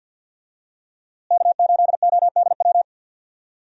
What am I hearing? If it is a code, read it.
Morse code, heard as a keyed tone, about 46 wpm, 700 Hz. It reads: K8ODW
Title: USA Call Signs - 5 Characters - Repeated 3x - 50wpm